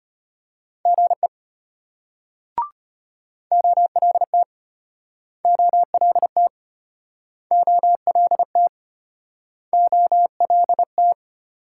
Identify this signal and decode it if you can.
Morse code, heard as a keyed tone, about 35 wpm, 700 Hz. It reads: GE OLT OLT OLT OLT